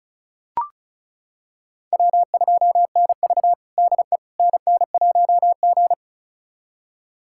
Morse code, heard as a keyed tone, about 36 wpm, 700 Hz. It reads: W2NV DE NN1G
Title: USA Callsign DE Callsign - 4 Characters - 35wpm